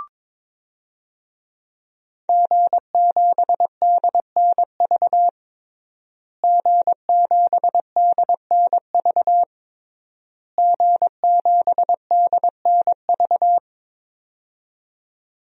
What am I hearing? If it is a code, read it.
Morse code, heard as a keyed tone, about 22 wpm, 700 Hz. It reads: G7DNV G7DNV G7DNV
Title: DXCC-100 Call Signs - 5 Characters - Repeated 3x - 22wpm